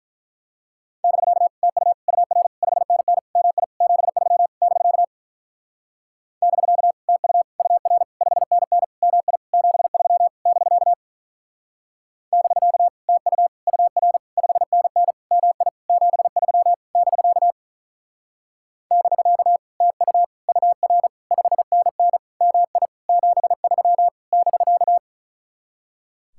Morse code, heard as a keyed tone, about 44 wpm, 700 Hz. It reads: <BK> TU UR 5NN MI 73 <BK> <BK> TU UR 5NN MI 73 <BK> <BK> TU UR 5NN MI 73 <BK> <BK> TU UR 5NN MI 73 <BK>